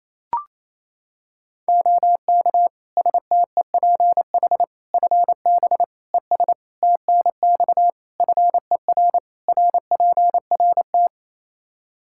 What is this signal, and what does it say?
Morse code, heard as a keyed tone, about 28 wpm, 700 Hz. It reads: OK STEPH FB ES TNX FER RPRT
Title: Practice QSO - Exchange 2 - Info - 28wpm